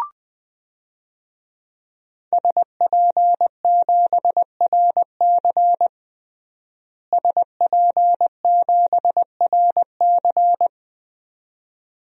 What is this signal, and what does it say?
Morse code, heard as a keyed tone, about 20 wpm, 700 Hz. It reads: SP7RC SP7RC
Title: DXCC-100 Call Signs - 5 Characters - Repeated 2x - 20wpm